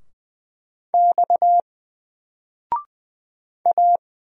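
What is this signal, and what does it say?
Morse code, heard as a keyed tone, about 20 wpm, 700 Hz. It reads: X A
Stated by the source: Single letter and number Morse drill at 20 wpm, from N4LCD.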